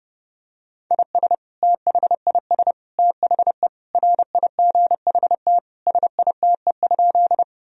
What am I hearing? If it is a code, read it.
Morse code, heard as a keyed tone, about 32 wpm, 700 Hz. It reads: IS THIS THE RIGHT SITE?
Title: Sentences from Top 1000 Words - Review Part 4 - 30wpm